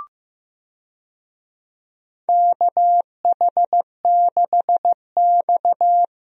Morse code, heard as a keyed tone, about 15 wpm, 700 Hz. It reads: KH6X
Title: DXCC-100 Call Signs - 4 Characters - Repeated 2x - 15wpm